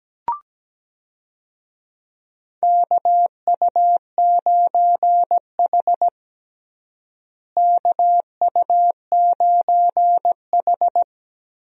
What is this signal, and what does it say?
Morse code, heard as a keyed tone, about 17 wpm, 700 Hz. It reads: KU9H KU9H
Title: USA Call Signs - 4 Characters - Repeated 2x - 17wpm